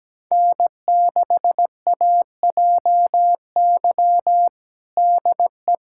Morse code, heard as a keyed tone, about 17 wpm, 700 Hz. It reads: N6AJY DE
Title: USA Callsign DE Callsign - 5 Characters - 17wpm